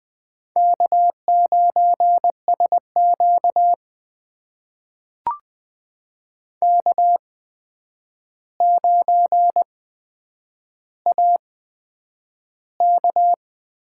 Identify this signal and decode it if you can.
Morse code, heard as a keyed tone, about 20 wpm, spaced as for 9 wpm, 700 Hz. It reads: K9SQ K9AK
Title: USA Call Signs - 4 Characters - 8x Character Spacing - 20wpm